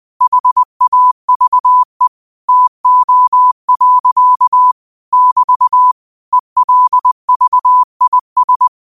Morse code, heard as a keyed tone, about 20 wpm, 1000 Hz. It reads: HAVE TO. <BT> ELVIS